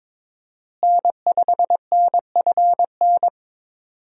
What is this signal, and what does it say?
Morse code, heard as a keyed tone, about 22 wpm, 700 Hz. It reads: N5NFN
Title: USA Call Signs - 5 Characters - Repeated 3x - 22wpm